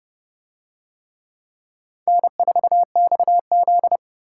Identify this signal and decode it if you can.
Morse code, heard as a keyed tone, about 28 wpm, 700 Hz. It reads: N4XZ